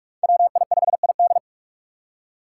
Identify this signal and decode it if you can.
Morse code, heard as a keyed tone, about 44 wpm, 700 Hz. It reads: WI5ID